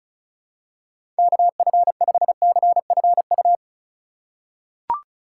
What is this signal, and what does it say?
Morse code, heard as a keyed tone, about 34 wpm, 700 Hz. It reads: KF5CFU
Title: USA Call Signs - 6 Characters - 2x Character Spacing - 35wpm